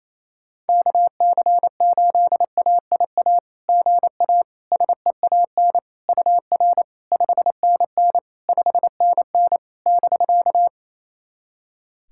Morse code, heard as a keyed tone, about 28 wpm, 700 Hz. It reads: KC8AIA GA SEAN UR 5NN 5NN <BK>
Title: SOTA QSO Practice 28wpm